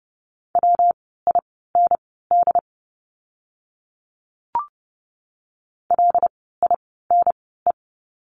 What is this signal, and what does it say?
Morse code, heard as a keyed tone, about 30 wpm, spaced as for 17 wpm, 700 Hz. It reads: WIND LINE
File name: course / 30:10wpm / 115 Lesson 19f - Letter w - Words 30wpm